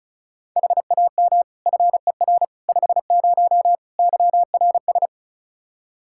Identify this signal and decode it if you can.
Morse code, heard as a keyed tone, about 36 wpm, 700 Hz. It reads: HAM FER 50 YRS
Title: Practice QSO - Exchange 3 - More Info - 35wpm